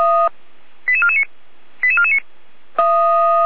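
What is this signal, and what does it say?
Five-tone selective call (ZVEI), heard as quick runs of five tones, tones of about 70 ms. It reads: 88449 88449
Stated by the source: ZVEI five-tone selective call received via SDR.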